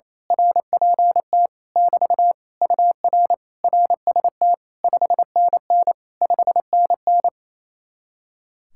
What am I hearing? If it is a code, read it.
Morse code, heard as a keyed tone, about 27 wpm, 700 Hz. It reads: RPT <BT> UR RST 5NN 5NN
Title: Speed-Racing Common QSO Phrases - 28wpm